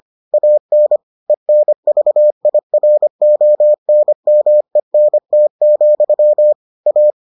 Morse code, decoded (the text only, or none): AN ENVIRONMENT, A